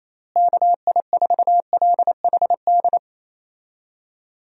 Morse code, heard as a keyed tone, about 28 wpm, 700 Hz. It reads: KI4LHD
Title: USA Call Signs - 6 Characters - Repeated 2x - 28wpm